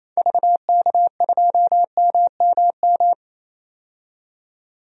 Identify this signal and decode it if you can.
Morse code, heard as a keyed tone, about 28 wpm, 700 Hz. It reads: VK2MMM
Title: DXCC-100 Call Signs - 6 Characters - 28wpm